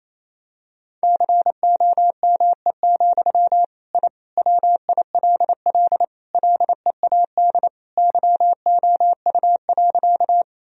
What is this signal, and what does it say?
Morse code, heard as a keyed tone, about 28 wpm, 700 Hz. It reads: COME, I WILL LEAD YOU.
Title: Sentences from Top 1000 Words - Review Part 2 - 28wpm